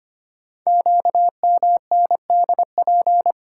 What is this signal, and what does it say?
Morse code, heard as a keyed tone, about 25 wpm, 700 Hz. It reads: QMNDP
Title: Code Groups - 5 Letters - 25wpm